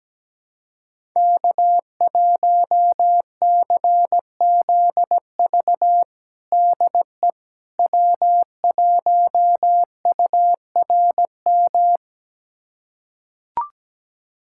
Morse code, heard as a keyed tone, about 17 wpm, 700 Hz. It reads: K1CZV DE W1URM
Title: USA Callsign DE Callsign - 5 Characters - 17wpm